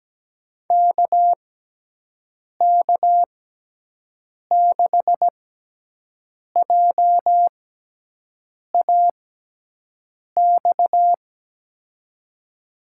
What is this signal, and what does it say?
Morse code, heard as a keyed tone, about 17 wpm, spaced as for 6 wpm, 700 Hz. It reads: KK6JAX
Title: USA Call Signs - 6 Characters - 6x Character Spacing - 17wpm